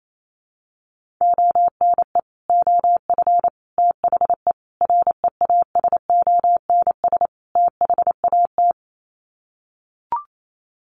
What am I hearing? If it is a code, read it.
Morse code, heard as a keyed tone, about 28 wpm, 700 Hz. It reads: ONE OF THE REASONS THAT
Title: Sets of 5 Words Encore 28wpm